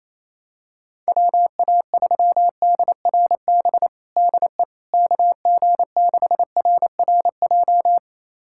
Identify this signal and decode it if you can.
Morse code, heard as a keyed tone, about 28 wpm, 700 Hz. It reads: WA3DRB DE KG6RRJ
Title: USA Callsign DE Callsign - 6 Characters - 28wpm